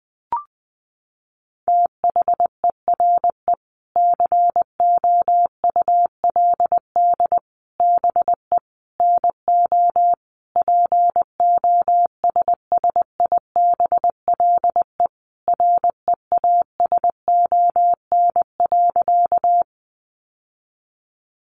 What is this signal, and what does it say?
Morse code, heard as a keyed tone, about 20 wpm, 700 Hz. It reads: THERE COULD BE NO POSSIBLE REASON.
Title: Sentences from Top 700 Words 20wpm